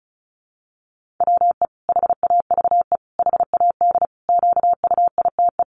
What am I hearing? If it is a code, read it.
Morse code, heard as a keyed tone, about 34 wpm, 700 Hz. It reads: WE HAVE HAD QUITE
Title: Sentences from Top 700 Words 35wpm